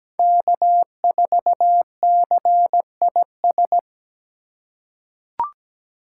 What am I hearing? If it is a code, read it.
Morse code, heard as a keyed tone, about 17 wpm, 700 Hz. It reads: K4CIS
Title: USA Call Signs - 2x Character Spacing - 17wpm